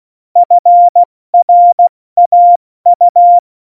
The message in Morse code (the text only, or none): FRAU